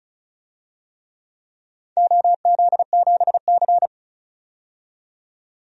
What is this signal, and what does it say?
Morse code, heard as a keyed tone, about 34 wpm, 700 Hz. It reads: OZ7C
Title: DXCC-100 Call Signs - 4 Characters - 35wpm